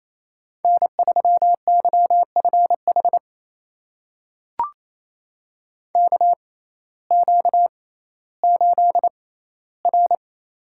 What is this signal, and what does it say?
Morse code, heard as a keyed tone, about 28 wpm, spaced as for 15 wpm, 700 Hz. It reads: N3YFH KQ8R
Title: USA Call Signs - 6x Character Spacing - 28wpm